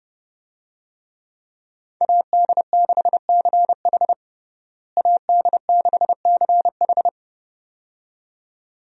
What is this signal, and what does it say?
Morse code, heard as a keyed tone, about 30 wpm, 700 Hz. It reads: AD6CH AD6CH
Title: USA Call Signs - 5 Characters - Repeated 2x - 30wpm